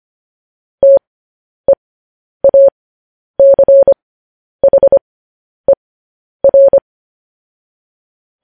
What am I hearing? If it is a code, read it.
Morse code, heard as a keyed tone, about 25 wpm, spaced as for 10 wpm, 550 Hz. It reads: TEACHER
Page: news headlines in morse code at 10 wpm